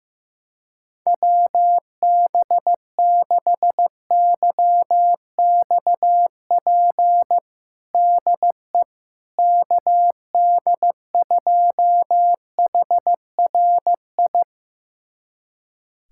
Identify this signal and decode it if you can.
Morse code, heard as a keyed tone, about 15 wpm, 700 Hz. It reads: WB6YXP DE KD2HRI
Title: USA Callsign DE Callsign 15wpm